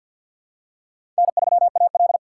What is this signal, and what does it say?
Morse code, heard as a keyed tone, about 51 wpm, 700 Hz. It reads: N3AL
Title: USA Call Signs - 4 Characters - 3x Character Spacing - 50wpm